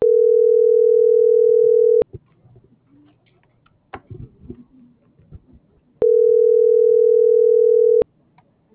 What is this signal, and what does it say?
Ringback tone (North America), 440 + 480 Hz.